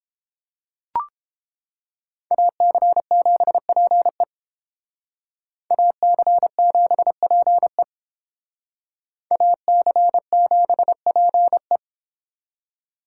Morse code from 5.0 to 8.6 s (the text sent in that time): AC7PE